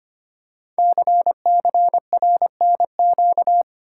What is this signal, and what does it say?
Morse code, heard as a keyed tone, about 25 wpm, 700 Hz. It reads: CCRNQ
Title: Code Groups - 5 Letter:Numbers - 25wpm